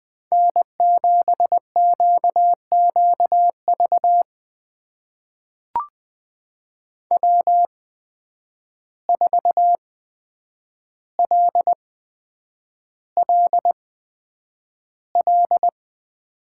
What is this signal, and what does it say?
Morse code, heard as a keyed tone, about 20 wpm, spaced as for 9 wpm, 700 Hz. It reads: N7QQV W4LLL
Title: USA Call Signs - 5 Characters - 8x Character Spacing - 20wpm